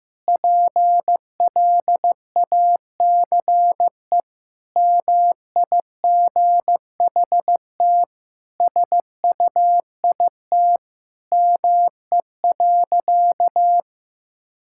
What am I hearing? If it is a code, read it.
Morse code, heard as a keyed tone, about 16 wpm, 700 Hz. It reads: PLACE MIGHT SUIT ME.
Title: Sentences from Top 1600 Words 15wpm